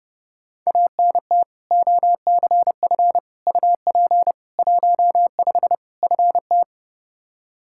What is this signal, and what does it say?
Morse code, heard as a keyed tone, about 32 wpm, 700 Hz. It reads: ANT OCF UP 15 FT
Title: Practice QSO - Exchange 2 - Info - 30wpm